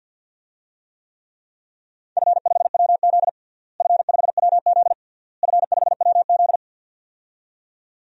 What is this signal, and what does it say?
Morse code, heard as a keyed tone, about 52 wpm, 700 Hz. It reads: F5WZ F5WZ F5WZ
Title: DXCC-100 Call Signs - 4 Characters - Repeated 3x - 50wpm